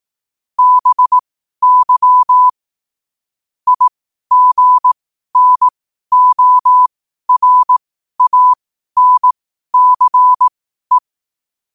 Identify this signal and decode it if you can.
Morse code, heard as a keyed tone, about 18 wpm, spaced as for 12 wpm, 1000 Hz. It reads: BY IGNORANCE
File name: qotdfiles / 12wpm